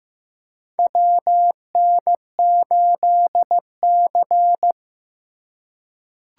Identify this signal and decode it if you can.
Morse code, heard as a keyed tone, about 15 wpm, 700 Hz. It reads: WN8C
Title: USA Call Signs - 4 Characters - 3x Character Spacing - 15wpm